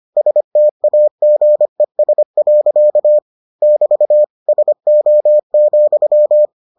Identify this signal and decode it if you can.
Morse code, heard as a keyed tone, about 25 wpm, 600 Hz. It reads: STAGES. <BT> SO,